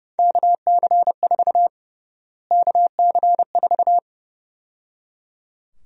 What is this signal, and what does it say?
Morse code, heard as a keyed tone, about 30 wpm, 700 Hz. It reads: KC4 KC4
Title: USA Call Sign Prefixes - Repeated 2x - 30wpm